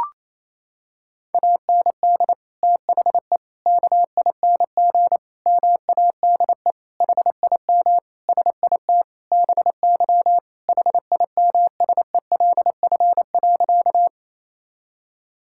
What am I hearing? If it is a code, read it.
Morse code, heard as a keyed tone, about 28 wpm, 700 Hz. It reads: AND THE KING MADE HIM SIT BY HIMSELF.
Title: Sentences from Top 1600 Words 28wpm